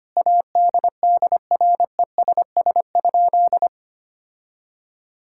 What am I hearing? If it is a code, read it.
Morse code, heard as a keyed tone, about 25 wpm, 700 Hz. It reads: ADDRESS?
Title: Sentences from Top 1000 Words - Review Part 6 - 25wpm